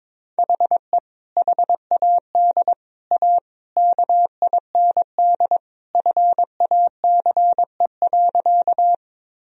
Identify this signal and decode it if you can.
Morse code, heard as a keyed tone, about 22 wpm, 700 Hz. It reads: HE HAD A KIND FACE.
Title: Sentences from Top 1000 Words - Review Part 2 - 22wpm